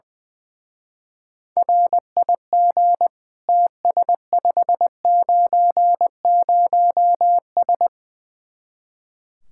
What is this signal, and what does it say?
Morse code, heard as a keyed tone, about 20 wpm, 700 Hz. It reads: RIG TS590S
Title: Speed-Racing Common QSO Phrases - 20wpm